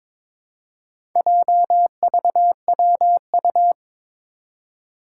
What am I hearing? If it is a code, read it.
Morse code, heard as a keyed tone, about 23 wpm, 700 Hz. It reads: JVWU
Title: Four Challenging Letter-Number Combinations 22wpm